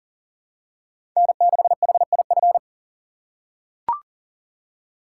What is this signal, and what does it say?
Morse code, heard as a keyed tone, about 40 wpm, 700 Hz. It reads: N6HIF
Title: USA Call Signs - 5 Characters - 6x Character Spacing - 40wpm